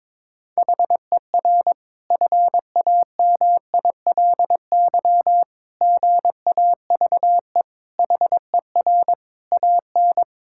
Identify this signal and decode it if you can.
Morse code, heard as a keyed tone, about 21 wpm, 700 Hz. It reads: HER FAMILY GAVE HER AN
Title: Sentences from Top 1000 Words - Review Part 5 - 22wpm